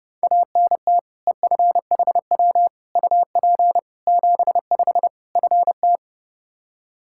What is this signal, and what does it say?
Morse code, heard as a keyed tone, about 29 wpm, 700 Hz. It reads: ANT EFHW UP 75 FT
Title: Practice QSO - Exchange 2 - Info - 30wpm